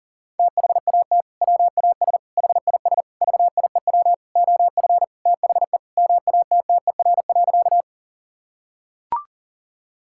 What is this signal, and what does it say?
Morse code, heard as a keyed tone, about 40 wpm, 700 Hz. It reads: THAT WAS HIS VIEW OF THE MATTER.
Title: Sentences from Top 1000 Words - Review Part 6 - 40wpm